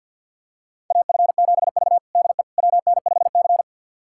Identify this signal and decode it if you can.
Morse code, heard as a keyed tone, about 46 wpm, 700 Hz. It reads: AF7V DE WN5C
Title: USA Callsign DE Callsign - 4 Characters - 50wpm